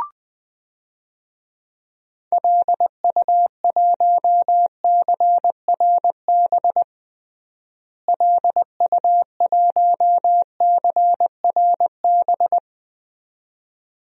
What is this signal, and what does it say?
Morse code, heard as a keyed tone, about 20 wpm, 700 Hz. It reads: LU1CRB LU1CRB
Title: DXCC-100 Call Signs - 6 Characters - Repeated 2x - 20wpm